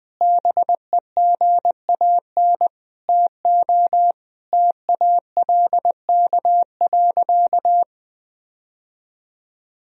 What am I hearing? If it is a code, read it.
Morse code, heard as a keyed tone, about 21 wpm, 700 Hz. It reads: BEGAN TO TALK.
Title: Sentences from Top 1000 Words - Review Part 1 - 20wpm